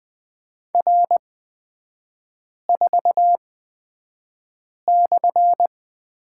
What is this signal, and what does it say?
Morse code, heard as a keyed tone, about 20 wpm, spaced as for 5 wpm, 700 Hz. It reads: R4/
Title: 250 Lesson 40e - BK - Three Characters 20wpm